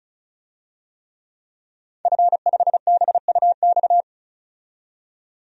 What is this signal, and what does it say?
Morse code, heard as a keyed tone, about 34 wpm, 700 Hz. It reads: F5BUX
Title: DXCC-100 Call Signs - 5 Characters - 35wpm